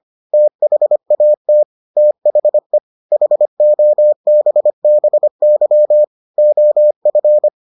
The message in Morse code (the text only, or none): THAT THE HOBBY OF